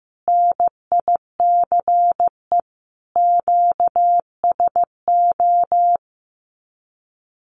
Morse code, heard as a keyed tone, about 15 wpm, 700 Hz. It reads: NICE QSO
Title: Common QSO Phrases 15wpm